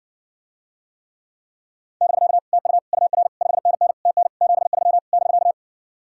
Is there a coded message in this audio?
Yes: Morse code from 2.0 to 5.5 s, heard as a keyed tone, about 60 wpm, 700 Hz. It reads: <BK> TU UR 5NN TN 73 <BK>